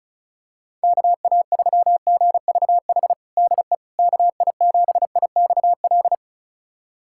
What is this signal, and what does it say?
Morse code, heard as a keyed tone, about 34 wpm, 700 Hz. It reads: KA3GVH DE KI7IXL
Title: USA Callsign DE Callsign - 6 Characters - 35wpm